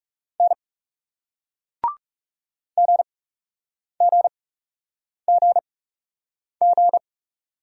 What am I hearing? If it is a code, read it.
Morse code, heard as a keyed tone, about 40 wpm, spaced as for 8 wpm, 700 Hz. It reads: N GGGG